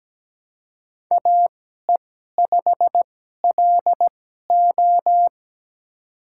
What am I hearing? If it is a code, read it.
Morse code, heard as a keyed tone, about 17 wpm, spaced as for 12 wpm, 700 Hz. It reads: AE5LO
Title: USA Call Signs - 5 Characters - 2x Character Spacing - 17wpm